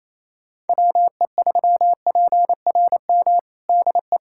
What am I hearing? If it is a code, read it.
Morse code, heard as a keyed tone, about 28 wpm, 700 Hz. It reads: WE3PRM DE
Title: USA Callsign DE Callsign - 6 Characters - 28wpm